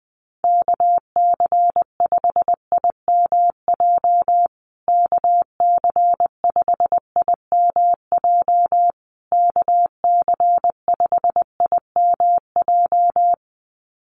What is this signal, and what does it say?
Morse code, heard as a keyed tone, about 20 wpm, 700 Hz. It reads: KC5IMJ KC5IMJ KC5IMJ
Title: USA Call Signs Repeated 3x 20wpm